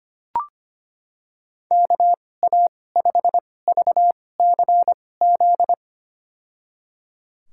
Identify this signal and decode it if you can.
Morse code, heard as a keyed tone, about 25 wpm, spaced as for 18 wpm, 700 Hz. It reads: KA5VCZ
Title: USA Call Signs - 6 Characters - 2x Character Spacing - 25wpm